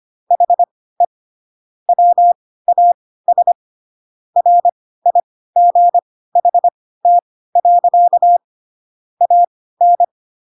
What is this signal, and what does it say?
Morse code, heard as a keyed tone, about 25 wpm, spaced as for 16 wpm, 700 Hz. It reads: HE WAS RIGHT. AN
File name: Pinocchio / Pinocchio ch6 10 25 ik1hge